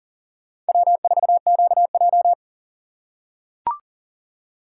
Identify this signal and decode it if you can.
Morse code, heard as a keyed tone, about 40 wpm, 700 Hz. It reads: W4QJ
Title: USA Call Signs - 4x Character Spacing - 40wpm